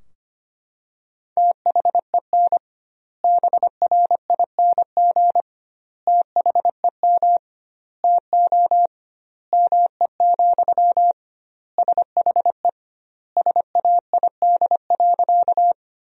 Morse code, heard as a keyed tone, about 25 wpm, 700 Hz. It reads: THEN BRING THEM TO ME, SHE SAID.
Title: 2x Word-Spacing - Sentences from Top 500 Words - 25wpm